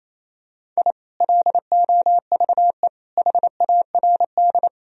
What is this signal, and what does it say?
Morse code, heard as a keyed tone, about 28 wpm, 700 Hz. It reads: I LOVE HARD
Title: Sentences from Top 1000 Words - Review Part 1 - 28wpm